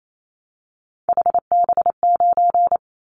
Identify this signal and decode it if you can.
Morse code, heard as a keyed tone, about 28 wpm, 700 Hz. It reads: HB9
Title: Callsign Prefixes for Top 25 DXCC - 28wpm